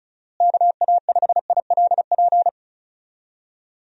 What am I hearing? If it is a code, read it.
Morse code, heard as a keyed tone, about 36 wpm, 700 Hz. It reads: KA5ILP
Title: USA Call Signs - 6 Characters - Repeated 2x - 35wpm